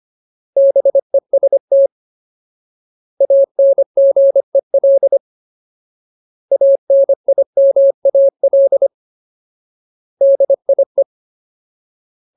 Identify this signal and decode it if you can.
Morse code, heard as a keyed tone, about 25 wpm, 550 Hz. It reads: BEST ANGEL ANIMAL DIE
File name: Ultimate Word List / 25WPM Ultimate